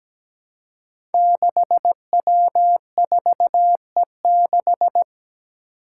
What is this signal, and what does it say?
Morse code, heard as a keyed tone, about 17 wpm, 700 Hz. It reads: 6W4E6
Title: Five Challenging Letter-Number Combinations 17wpm